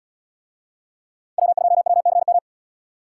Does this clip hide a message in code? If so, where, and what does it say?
Yes: Morse code from 1.4 to 2.4 s, heard as a keyed tone, about 76 wpm, 700 Hz. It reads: W2WCM